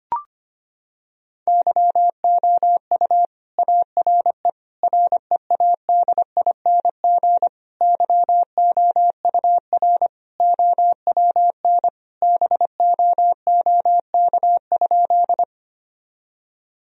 Morse code, heard as a keyed tone, about 25 wpm, 700 Hz. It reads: YOU ARE READING YOUR OWN BOOK?
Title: Sentences from Top 1000 Words - Review Part 3 - 25wpm